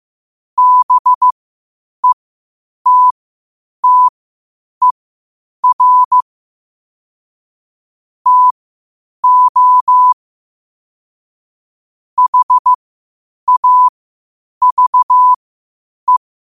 Morse code, heard as a keyed tone, about 15 wpm, spaced as for 8 wpm, 1000 Hz. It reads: BETTER TO HAVE
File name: qotdfiles / 8wpm